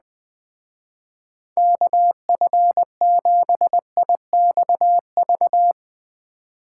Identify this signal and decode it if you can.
Morse code, heard as a keyed tone, about 20 wpm, 700 Hz. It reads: KF7IXV